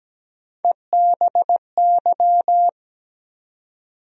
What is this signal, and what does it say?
Morse code, heard as a keyed tone, about 17 wpm, 700 Hz. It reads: EBY
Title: USA Call Sign Postfixes - 3 Letters 17wpm